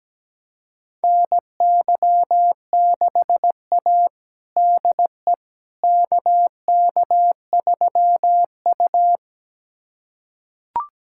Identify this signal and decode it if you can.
Morse code, heard as a keyed tone, about 17 wpm, 700 Hz. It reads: NY6A DE KK3U
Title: USA Callsign DE Callsign - 4 Characters - 17wpm